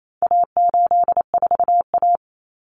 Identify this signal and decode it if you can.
Morse code, heard as a keyed tone, about 28 wpm, 700 Hz. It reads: A84A
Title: Common Cut Numbers - 4-Digit 28wpm